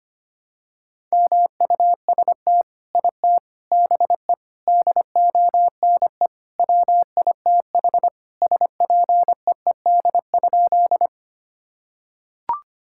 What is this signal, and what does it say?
Morse code, heard as a keyed tone, about 25 wpm, 700 Hz. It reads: MUST IT BE DONE WITH SPEED?
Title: Sentences from Top 1000 Words - Review Part 5 - 25wpm